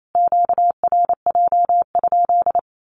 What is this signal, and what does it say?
Morse code, heard as a keyed tone, about 28 wpm, 700 Hz. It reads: QRJ?
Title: Q-Codes Repeated 3x 28wpm